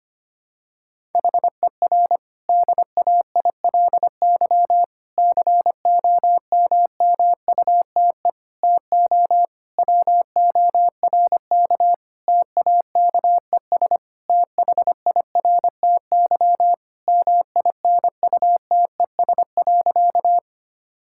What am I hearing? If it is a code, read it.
Morse code, heard as a keyed tone, about 25 wpm, 700 Hz. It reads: HER DAILY COMMUTE TO WORK TAKES THIRTY MINUTES.
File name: practice / 25wpm / Difficult Sentences with Hints - 25wpm